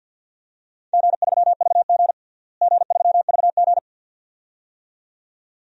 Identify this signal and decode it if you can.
Morse code, heard as a keyed tone, about 52 wpm, 700 Hz. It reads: G3VG G3VG
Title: DXCC-100 Call Signs - 4 Characters - Repeated 2x - 50wpm